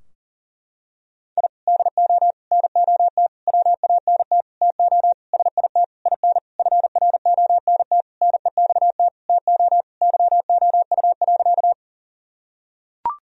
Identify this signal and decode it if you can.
Morse code, heard as a keyed tone, about 40 wpm, 700 Hz. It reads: I DO NOT WANT TO SIT IN FRONT NEXT TO YOU.
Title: Sentences from Top 1000 Words - Review Part 2 - 40wpm